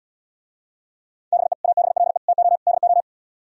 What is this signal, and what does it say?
Morse code, heard as a keyed tone, about 77 wpm, 700 Hz. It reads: BE ABLE TO DO